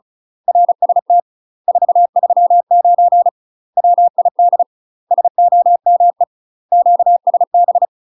Morse code, heard as a keyed tone, about 36 wpm, 700 Hz. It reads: RST 439 WID SOME QSB
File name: UK MASTER QSOs TRAINING/35WPM